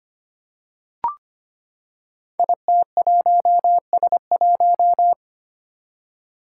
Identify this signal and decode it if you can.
Morse code, heard as a keyed tone, about 25 wpm, 700 Hz. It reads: IT1S1
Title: Five Challenging Letter-Number Combinations 25wpm